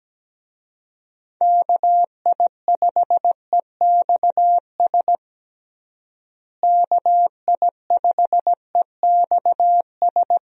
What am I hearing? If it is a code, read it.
Morse code, heard as a keyed tone, about 17 wpm, 700 Hz. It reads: KI5EXS KI5EXS
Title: USA Call Signs - 6 Characters - Repeated 3x - 17wpm